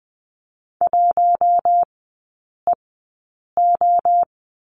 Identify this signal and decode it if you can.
Morse code, heard as a keyed tone, about 20 wpm, spaced as for 8 wpm, 700 Hz. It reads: 1EO
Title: 067 Lesson 12e - Letter d - Three Characters 20wpm